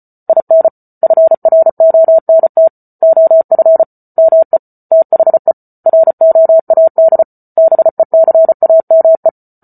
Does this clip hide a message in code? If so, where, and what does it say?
Yes: Morse code from 0.3 to 9.3 s, heard as a keyed tone, about 34 wpm, 650 Hz. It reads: IN FRONT OF ME THE ROAD BECAME